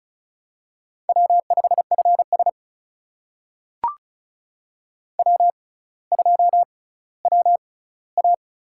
Morse code, heard as a keyed tone, about 36 wpm, spaced as for 18 wpm, 700 Hz. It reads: W5FS W2WA
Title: USA Call Signs - 4 Characters - 6x Character Spacing - 35wpm